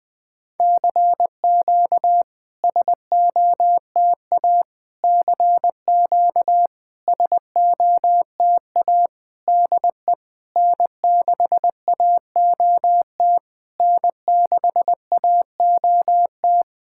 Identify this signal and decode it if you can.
Morse code, heard as a keyed tone, about 20 wpm, 700 Hz. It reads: CQ SOTA CQ SOTA DE N6AOT N6AOT
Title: SOTA QSO Practice 20wpm